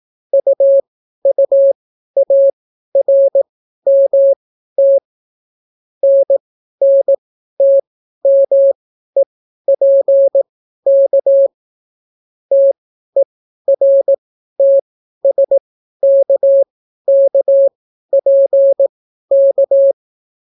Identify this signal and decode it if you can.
Morse code, heard as a keyed tone, about 18 wpm, spaced as for 12 wpm, 550 Hz. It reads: UUARMT NNTMEPK TERTSKKPK